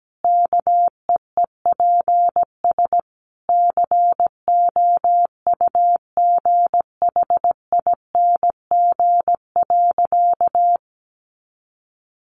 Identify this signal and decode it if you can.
Morse code, heard as a keyed tone, about 17 wpm, 700 Hz. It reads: KEEPS COUGHING.
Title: Additional Easy Phrases 17wpm